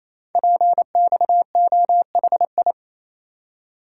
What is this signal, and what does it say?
Morse code, heard as a keyed tone, about 28 wpm, 700 Hz. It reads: PXOHI
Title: Code Groups - 5 Letters - 28wpm